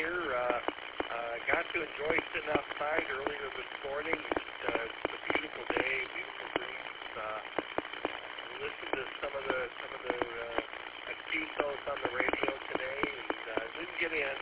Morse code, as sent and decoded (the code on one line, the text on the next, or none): none